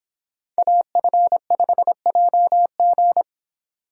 Morse code, heard as a keyed tone, about 29 wpm, 700 Hz. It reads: AF5JG